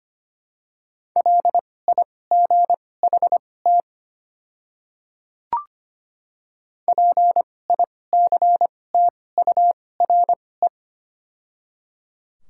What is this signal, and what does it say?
Morse code, heard as a keyed tone, about 25 wpm, spaced as for 18 wpm, 700 Hz. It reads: LIGHT PICTURE
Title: 2x Character Spacing - Top 300 Words - 25wpm